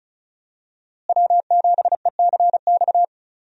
Morse code, heard as a keyed tone, about 36 wpm, 700 Hz. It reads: W7ECX
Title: USA Call Signs - 5 Characters - Repeated 2x - 35wpm